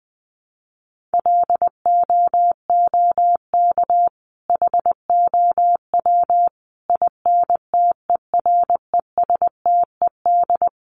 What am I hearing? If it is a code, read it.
Morse code, heard as a keyed tone, about 20 wpm, 700 Hz. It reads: LOOK HOW INTERESTED
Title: Sentences from Top 1400 Words 20wpm